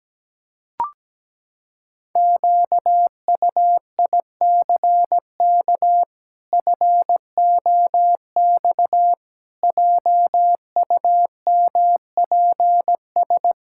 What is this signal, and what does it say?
Morse code, heard as a keyed tone, about 17 wpm, 700 Hz. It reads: QUICK FOX JUMPS
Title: Pangram Challenge v2 - 17wpm